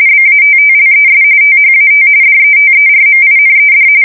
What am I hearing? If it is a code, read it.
RTTY, about 45.45 baud. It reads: BATTLES THAT DETERMINED
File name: fsk170 / USS-New-Jersey